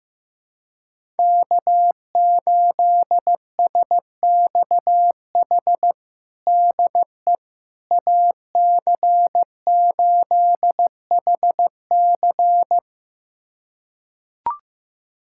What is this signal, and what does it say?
Morse code, heard as a keyed tone, about 15 wpm, 700 Hz. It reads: K8SXH DE AC8HC
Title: USA Callsign DE Callsign 15wpm